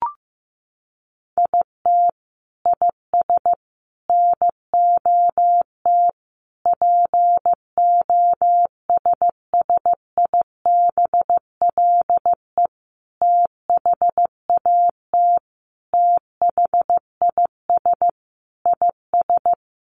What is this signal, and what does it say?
Morse code, heard as a keyed tone, about 15 wpm, 700 Hz. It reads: IT IS NOT POSSIBLE THAT THIS IS
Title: Sentences from Top 1200 Words 15wpm